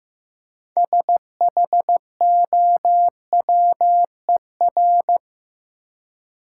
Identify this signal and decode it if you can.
Morse code, heard as a keyed tone, about 15 wpm, 700 Hz. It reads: SHOWER
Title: Six Letter Words - 15wpm